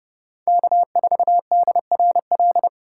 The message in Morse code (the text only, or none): K4DRL